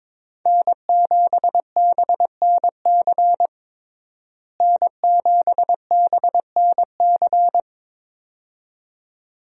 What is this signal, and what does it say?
Morse code, heard as a keyed tone, about 21 wpm, 700 Hz. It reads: N7BNC N7BNC
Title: USA Call Signs - 5 Characters - Repeated 2x - 22wpm